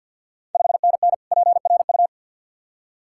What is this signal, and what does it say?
Morse code, heard as a keyed tone, about 51 wpm, 690 Hz. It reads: 5NN PRU
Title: Ontario QSO Party Exchange 50wpm